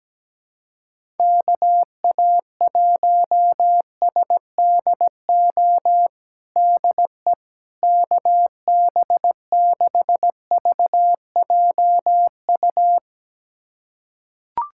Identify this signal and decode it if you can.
Morse code, heard as a keyed tone, about 17 wpm, 700 Hz. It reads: KA1SDO DE KB6VJU
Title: USA Callsign DE Callsign - 6 Characters - 17wpm